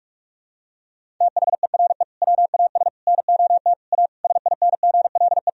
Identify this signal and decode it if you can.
Morse code, heard as a keyed tone, about 46 wpm, 700 Hz. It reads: THERE WAS NOT A SINGLE
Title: Sentences from Top 1000 Words - Review Part 4 - 45wpm